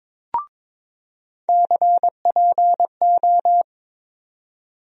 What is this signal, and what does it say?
Morse code, heard as a keyed tone, about 21 wpm, 700 Hz. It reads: CPO
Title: USA Call Sign Postfixes - 3 Letters 22wpm